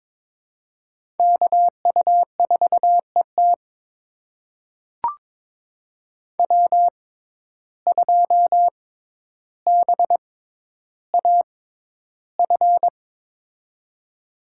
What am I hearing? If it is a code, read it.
Morse code, heard as a keyed tone, about 22 wpm, spaced as for 11 wpm, 700 Hz. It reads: KU4ET W2BAF
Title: USA Call Signs - 5 Characters - 6x Character Spacing - 22wpm